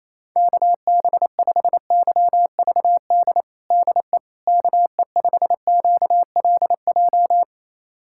Morse code, heard as a keyed tone, about 28 wpm, 700 Hz. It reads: KB5YVD DE KE5QLJ
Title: USA Callsign DE Callsign - 6 Characters - 28wpm